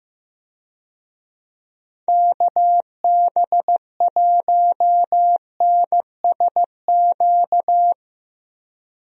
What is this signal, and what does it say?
Morse code, heard as a keyed tone, about 15 wpm, 700 Hz. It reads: KB1NSQ
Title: USA Call Signs - 6 Characters - 15wpm